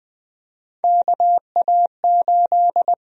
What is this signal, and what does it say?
Morse code, heard as a keyed tone, about 20 wpm, 700 Hz. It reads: KA8
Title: USA Call Sign Prefixes - 20wpm